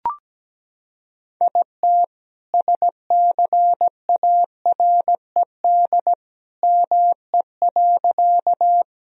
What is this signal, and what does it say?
Morse code, heard as a keyed tone, about 17 wpm, 700 Hz. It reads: IT SCARED ME.
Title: Sentences from Top 1000 Words - Review Part 4 - 17wpm